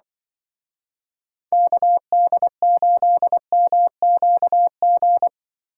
Morse code, heard as a keyed tone, about 23 wpm, 700 Hz. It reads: KD8MQG
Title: Speed Racing - USA 2x3 Call Signs 20wpm